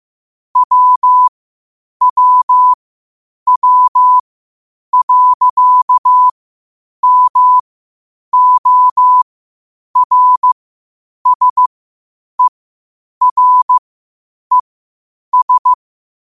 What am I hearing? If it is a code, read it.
Morse code, heard as a keyed tone, about 15 wpm, spaced as for 8 wpm, 1000 Hz. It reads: WWW.MORSERES
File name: qotdfiles / 8wpm